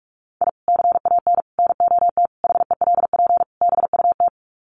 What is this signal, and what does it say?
Morse code, heard as a keyed tone, about 44 wpm, 700 Hz. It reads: I CAN NOT HELP BUT
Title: Sets of 5 Words Encore2 45wpm